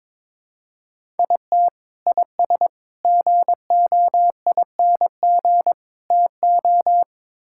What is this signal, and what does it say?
Morse code, heard as a keyed tone, about 22 wpm, 700 Hz. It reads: IT IS GOING TO
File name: practice / 22wpm / Build A Phrase - 5-Word Phrases - 22wpm